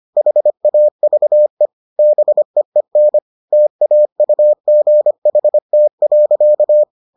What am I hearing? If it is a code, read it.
Morse code, heard as a keyed tone, about 25 wpm, 600 Hz. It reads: HAVE BEEN TAUGHT.